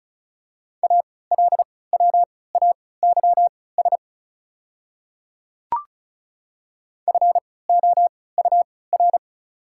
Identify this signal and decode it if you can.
Morse code, heard as a keyed tone, about 34 wpm, spaced as for 20 wpm, 690 Hz. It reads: ALWAYS FOUR
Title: 3x Character Spacing - Top 300 Words - 35wpm